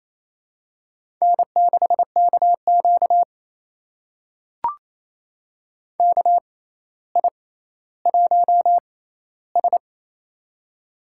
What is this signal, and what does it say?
Morse code, heard as a keyed tone, about 27 wpm, spaced as for 14 wpm, 700 Hz. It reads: N6KQ KI1S
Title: USA Call Signs - 4 Characters - 6x Character Spacing - 28wpm